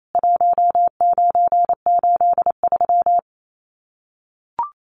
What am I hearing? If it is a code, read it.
Morse code, heard as a keyed tone, about 28 wpm, 700 Hz. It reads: 1983
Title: Common QSO Phrases 28wpm